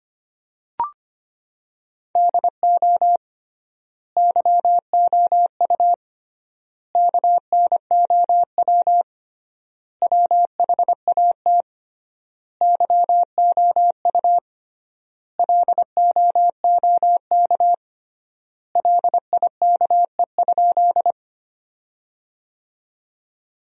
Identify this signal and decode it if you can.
Morse code, heard as a keyed tone, about 25 wpm, 700 Hz. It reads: DO YOU KNOW WHAT YOU LOOK LIKE?
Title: 3x Word-Spacing - Sentences from Top 100 Words - 25wpm